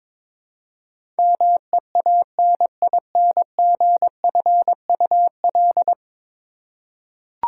Morse code, heard as a keyed tone, about 22 wpm, 700 Hz. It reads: MEANINGFUL
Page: morse-code-ninja-practice-22wpm